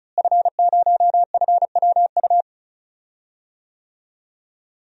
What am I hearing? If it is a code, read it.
Morse code, heard as a keyed tone, about 36 wpm, 700 Hz. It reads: F0FWU
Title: DXCC-100 Call Signs - 5 Characters - Repeated 2x - 35wpm